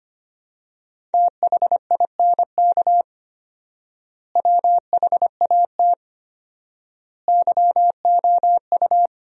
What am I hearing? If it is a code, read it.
Morse code, heard as a keyed tone, about 25 wpm, 700 Hz. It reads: THINK WHAT YOU
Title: 4x Word-Spacing - Sentences from Top 200 Words - 25wpm